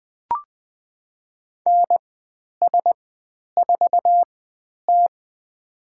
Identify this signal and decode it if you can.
Morse code, heard as a keyed tone, about 20 wpm, spaced as for 10 wpm, 700 Hz. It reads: NS4T
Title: 109 Lesson 18g - Letter m - Call Signs 20wpm